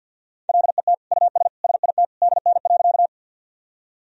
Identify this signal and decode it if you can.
Morse code, heard as a keyed tone, about 45 wpm, 700 Hz. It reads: LET US SIT DN.
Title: Sentences from Top 500 Words and Abbreviations 50wpm